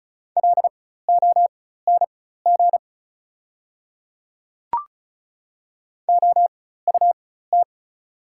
Morse code, heard as a keyed tone, about 34 wpm, spaced as for 16 wpm, 690 Hz. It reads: LONG OUT
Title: 4x Character Spacing - Top 300 Words - 35wpm